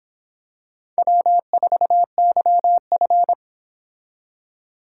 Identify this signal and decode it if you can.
Morse code, heard as a keyed tone, about 26 wpm, 700 Hz. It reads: W4YF